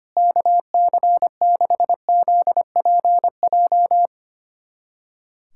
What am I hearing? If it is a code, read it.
Morse code, heard as a keyed tone, about 25 wpm, 700 Hz. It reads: KC6ZPJ
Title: USA Callsign DE Callsign - 6 Characters - 25wpm